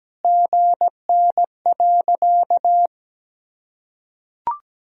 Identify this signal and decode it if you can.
Morse code, heard as a keyed tone, about 17 wpm, 700 Hz. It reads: GN.
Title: Sentences from Top 300 Words and Abbreviations 17wpm